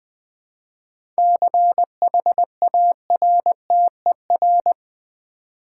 Morse code, heard as a keyed tone, about 20 wpm, 700 Hz. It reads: CHARTER
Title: 7 Letter Words - First 3 Letters Given - 20wpm